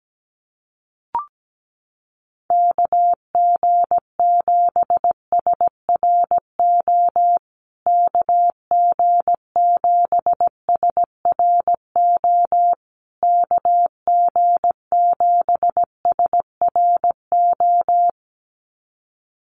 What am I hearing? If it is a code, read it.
Morse code, heard as a keyed tone, about 17 wpm, 700 Hz. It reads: KG7SRO KG7SRO KG7SRO
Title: USA Call Signs Repeated 3x 17wpm